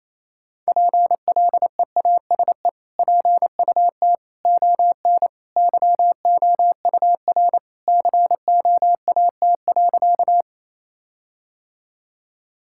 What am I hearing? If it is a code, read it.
Morse code, heard as a keyed tone, about 28 wpm, 700 Hz. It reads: PLEASE PUT ON YOUR COAT.
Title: Sentences from Top 1600 Words 28wpm